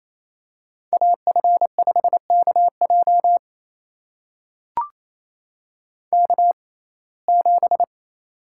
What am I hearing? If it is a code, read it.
Morse code, heard as a keyed tone, about 28 wpm, spaced as for 20 wpm, 700 Hz. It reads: AF5KJ K7
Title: USA Call Signs - 5 Characters - 6x Character Spacing - 28wpm